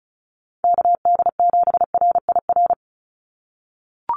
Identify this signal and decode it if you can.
Morse code, heard as a keyed tone, about 34 wpm, 700 Hz. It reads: KD7RIR
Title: USA Call Signs 35wpm